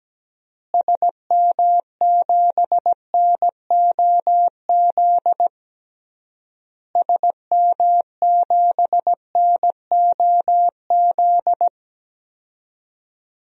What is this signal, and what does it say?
Morse code, heard as a keyed tone, about 17 wpm, 700 Hz. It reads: SM7NOZ SM7NOZ
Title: DXCC-100 Call Signs - 6 Characters - Repeated 3x - 17wpm